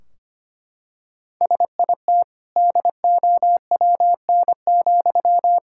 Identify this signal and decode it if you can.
Morse code, heard as a keyed tone, about 25 wpm, 700 Hz. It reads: SIT DOWN,
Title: Sentences from Top 1600 Words 25wpm